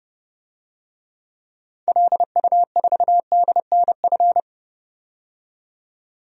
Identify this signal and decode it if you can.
Morse code, heard as a keyed tone, about 30 wpm, 700 Hz. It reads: LU4DNF
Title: DXCC-100 Call Signs - 6 Characters - 30wpm